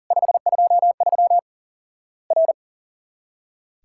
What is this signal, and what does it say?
Morse code, heard as a keyed tone, about 40 wpm, 700 Hz. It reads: 523 R
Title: Worked All Europe DX Contest - QTC Exchange - 40wpm